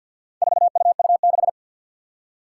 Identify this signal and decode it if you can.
Morse code, heard as a keyed tone, about 52 wpm, 700 Hz. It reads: 4UU6
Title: Four Challenging Letter-Number Combinations 50wpm